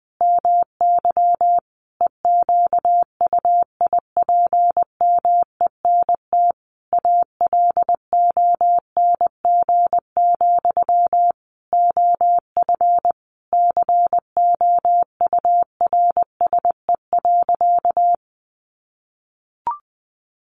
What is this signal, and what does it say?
Morse code, heard as a keyed tone, about 20 wpm, 700 Hz. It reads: MY EQUIPMENT ALONG, OF COURSE.
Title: Sentences from Top 700 Words 20wpm